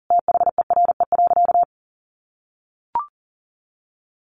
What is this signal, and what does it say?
Morse code, heard as a keyed tone, about 40 wpm, 700 Hz. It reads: THERE.
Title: Sentences from Top 1500 Words 40wpm